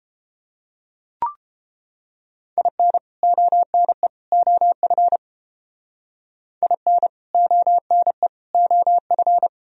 Morse code, heard as a keyed tone, about 32 wpm, 700 Hz. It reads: IN ONE OF IN ONE OF